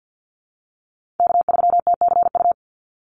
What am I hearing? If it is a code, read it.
Morse code, heard as a keyed tone, about 50 wpm, 700 Hz. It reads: K3TCU
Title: NA SOTA Chaser Callsigns 50wpm